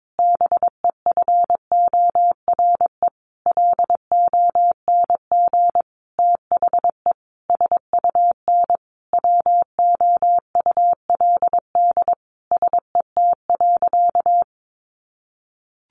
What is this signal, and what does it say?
Morse code, heard as a keyed tone, about 22 wpm, 700 Hz. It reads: BEFORE LONG THE SUN WOULD SET.
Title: Sentences from Top 400 Words 22wpm